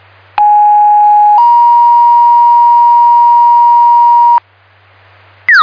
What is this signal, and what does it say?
Two-tone page, heard as a one-second tone, then a three-second tone, about 830 then 950 Hz. It is heard from 0.4 to 4.4 s.